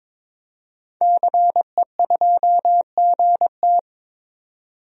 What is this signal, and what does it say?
Morse code, heard as a keyed tone, about 22 wpm, 700 Hz. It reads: CE2GT
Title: DXCC-100 Call Signs - 5 Characters - 22wpm